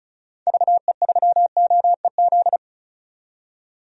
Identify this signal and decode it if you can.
Morse code, heard as a keyed tone, about 34 wpm, 700 Hz. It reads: VE3OEZ
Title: DXCC-100 Call Signs - 6 Characters - 35wpm